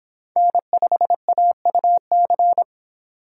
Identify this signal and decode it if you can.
Morse code, heard as a keyed tone, about 24 wpm, 700 Hz. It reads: N5AUC